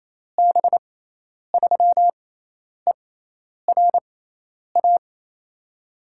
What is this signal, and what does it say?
Morse code, heard as a keyed tone, about 27 wpm, spaced as for 10 wpm, 700 Hz. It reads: B3ERA
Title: USA Call Signs - 6 Characters - 6x Character Spacing - 28wpm